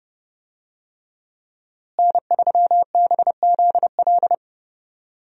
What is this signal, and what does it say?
Morse code, heard as a keyed tone, about 30 wpm, 700 Hz. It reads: N3BZL
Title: USA Call Signs - 5 Characters - Repeated 3x - 30wpm